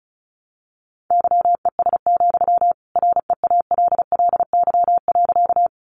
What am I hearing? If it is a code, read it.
Morse code, heard as a keyed tone, about 34 wpm, 700 Hz. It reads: YES, REALLY.
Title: Sentences from Top 600 Words 35wpm